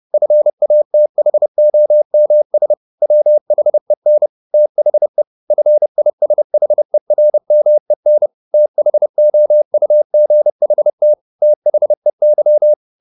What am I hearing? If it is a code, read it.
Morse code, heard as a keyed tone, about 30 wpm, 600 Hz. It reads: FATHOMS WHEN THE FISHERMEN THOUGHT THEY
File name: the old man and the sea / the old man and the sea 1 30wpm